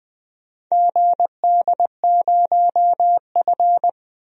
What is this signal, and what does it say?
Morse code, heard as a keyed tone, about 20 wpm, 700 Hz. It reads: GD0F
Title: DXCC-100 Call Signs - 4 Characters - 20wpm